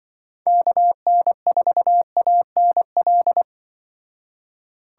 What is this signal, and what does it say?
Morse code, heard as a keyed tone, about 24 wpm, 700 Hz. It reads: KN4ANL
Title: Speed Racing - USA 2x3 Call Signs 20wpm